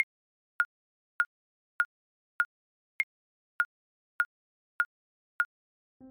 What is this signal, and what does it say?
There is no code.